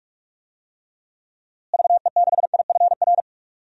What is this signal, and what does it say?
Morse code, heard as a keyed tone, about 46 wpm, 700 Hz. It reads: VE6IFR
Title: DXCC-100 Call Signs - 6 Characters - Repeated 3x - 45wpm